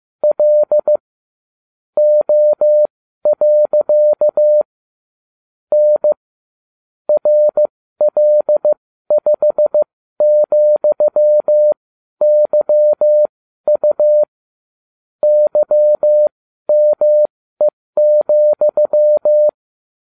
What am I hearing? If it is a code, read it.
Morse code, heard as a keyed tone, about 15 wpm, spaced as for 12 wpm, 600 Hz. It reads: L O. N RL5,YU YME,